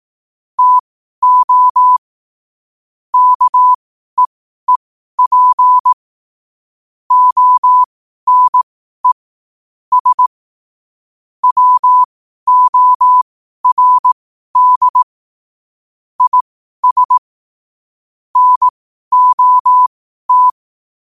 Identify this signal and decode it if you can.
Morse code, heard as a keyed tone, about 18 wpm, spaced as for 12 wpm, 1000 Hz. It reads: TO KEEP ONE S WORD IS NOT
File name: qotdfiles / 12wpm